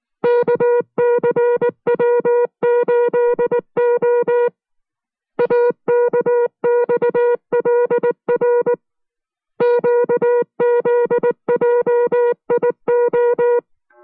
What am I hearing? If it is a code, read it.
Morse code, heard as a keyed tone, about 19 wpm, 480 Hz. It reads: KCW8O AKXLR QZJIO